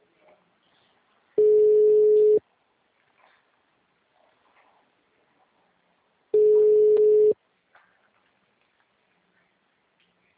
Ringback tone (Europe, 425 Hz).
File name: out- -2002- - - .70906